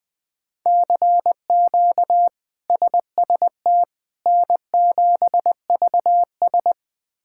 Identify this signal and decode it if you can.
Morse code, heard as a keyed tone, about 21 wpm, 700 Hz. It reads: CQ SST N7VS
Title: K1USN SST QSO Practice - 20wpm